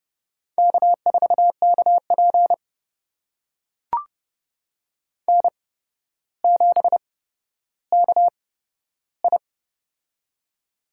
Morse code, heard as a keyed tone, about 30 wpm, spaced as for 13 wpm, 700 Hz. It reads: K4KP N7KI
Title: USA Call Signs - 4 Characters - 8x Character Spacing - 30wpm